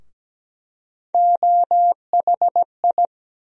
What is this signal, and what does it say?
Morse code, heard as a keyed tone, about 17 wpm, 700 Hz. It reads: OHI
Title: USA Call Sign Postfixes - 3 Letters 17wpm